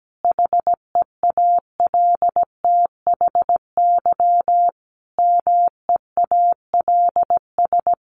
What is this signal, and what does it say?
Morse code, heard as a keyed tone, about 17 wpm, 700 Hz. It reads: HEALTHY MEALS
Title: Easy Phrases 17wpm